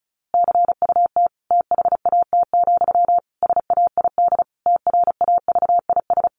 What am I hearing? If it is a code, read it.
Morse code, heard as a keyed tone, about 34 wpm, 700 Hz. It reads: CUT THAT, SAID TRAVIS
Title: A Sound Of Thunder 35wpm